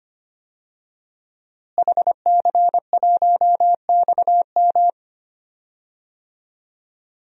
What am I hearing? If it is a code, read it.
Morse code, heard as a keyed tone, about 25 wpm, 700 Hz. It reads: HC1XM
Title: DXCC-100 Call Signs - 5 Characters - Repeated 3x - 25wpm